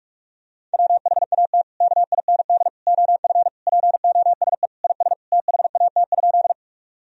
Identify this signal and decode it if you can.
Morse code, heard as a keyed tone, about 46 wpm, 700 Hz. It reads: WHAT KIND OF POSE IS THAT?
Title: Sentences from Top 1000 Words - Review Part 2 - 45wpm